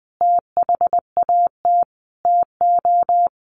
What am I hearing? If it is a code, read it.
Morse code, heard as a keyed tone, about 20 wpm, 700 Hz. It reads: THAT TO
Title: Sentences from Top 900 Words 20wpm